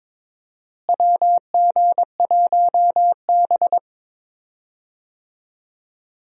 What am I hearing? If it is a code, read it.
Morse code, heard as a keyed tone, about 22 wpm, 700 Hz. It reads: WG1B
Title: USA Call Signs - 4 Characters - Repeated 2x - 22wpm